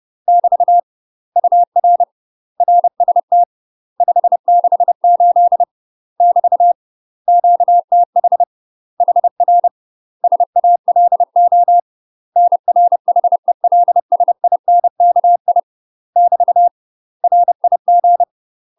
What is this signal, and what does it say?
Morse code, heard as a keyed tone, about 30 wpm, 700 Hz. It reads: <BT> UR RST 568 <BT> QTH HR SALO NR HELSINKI <BT> RIG
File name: UK MASTER QSOs TRAINING / 30WPM / UK MASTER QSOs 0042